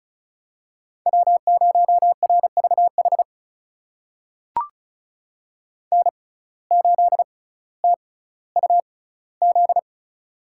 Morse code, heard as a keyed tone, about 36 wpm, spaced as for 18 wpm, 700 Hz. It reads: W0RVH N8TUZ
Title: USA Call Signs - 5 Characters - 6x Character Spacing - 35wpm